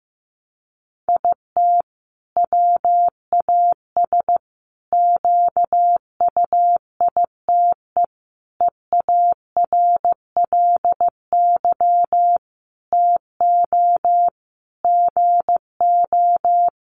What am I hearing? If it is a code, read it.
Morse code, heard as a keyed tone, about 15 wpm, 700 Hz. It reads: IT WAS QUITE EARLY TO GO
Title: Sentences from Top 700 Words 15wpm